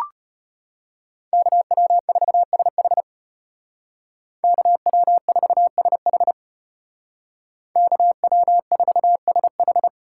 Morse code, heard as a keyed tone, about 31 wpm, 700 Hz. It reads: KW4SH KW4SH KW4SH
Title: Speed Racing - USA 2x2 Call Signs 25wpm